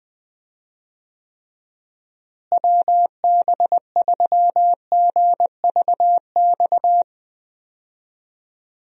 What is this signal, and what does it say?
Morse code, heard as a keyed tone, about 20 wpm, 700 Hz. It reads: WB3GVX
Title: USA Call Signs - 6 Characters - Repeated 2x - 20wpm